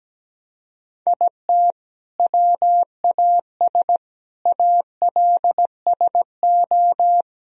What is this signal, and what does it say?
Morse code, heard as a keyed tone, about 17 wpm, 700 Hz. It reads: IT WAS ALSO
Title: Sentences from Top 1000 Words - Review Part 2 - 17wpm